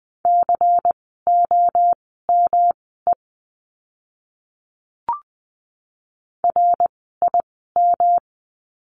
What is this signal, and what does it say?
Morse code, heard as a keyed tone, about 20 wpm, spaced as for 14 wpm, 700 Hz. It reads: COME RIM
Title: 108 Lesson 18f - Letter m - Words 20wpm